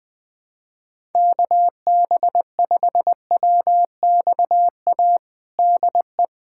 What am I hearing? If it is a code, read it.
Morse code, heard as a keyed tone, about 20 wpm, 700 Hz. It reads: KB5WXA DE
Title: USA Callsign DE Callsign - 6 Characters - 20wpm